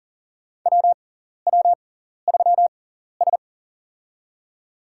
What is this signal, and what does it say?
Morse code, heard as a keyed tone, about 40 wpm, spaced as for 14 wpm, 700 Hz. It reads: WW3S
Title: USA Call Signs - 6x Character Spacing - 40wpm